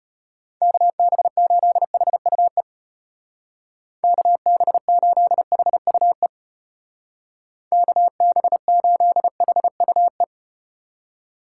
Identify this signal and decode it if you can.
Morse code, heard as a keyed tone, about 34 wpm, 700 Hz. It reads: KB8HUE KB8HUE KB8HUE